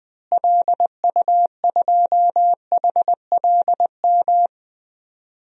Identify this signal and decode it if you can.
Morse code, heard as a keyed tone, about 20 wpm, 700 Hz. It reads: LU2HLM
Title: DXCC-100 Call Signs - 6 Characters - Repeated 3x - 20wpm